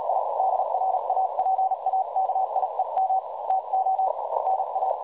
Morse code, heard as a keyed tone, about 37 wpm, 760 Hz, with noise. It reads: I DRAPAKIEM TO NA
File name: dlazaawansowanych / nagraniazpasma / websdr recording 3565.0kHz